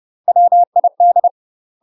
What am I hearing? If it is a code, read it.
Morse code, heard as a keyed tone, about 30 wpm, 700 Hz. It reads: WID
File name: UK MASTER QSOs TRAINING / 30WPM / UK MASTER QSOs 0022